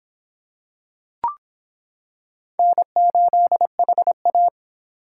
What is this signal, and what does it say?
Morse code, heard as a keyed tone, about 26 wpm, 700 Hz. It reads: N8HA